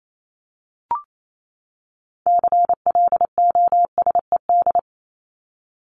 Morse code, heard as a keyed tone, about 28 wpm, 700 Hz. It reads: CLOSED
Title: ED Suffix - Top 100 Words with Base Words 28wpm